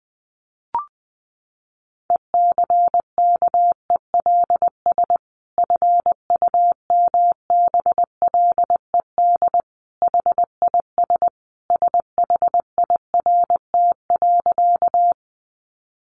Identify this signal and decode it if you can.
Morse code, heard as a keyed tone, about 20 wpm, 700 Hz. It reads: ECKELS FUMBLED HIS SHIRT.
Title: A Sound Of Thunder 20wpm